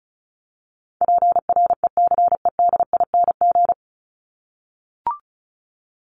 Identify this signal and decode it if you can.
Morse code, heard as a keyed tone, about 34 wpm, 700 Hz. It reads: PRECEDING
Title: ING Suffix - Top 100 Words with Base Words 35wpm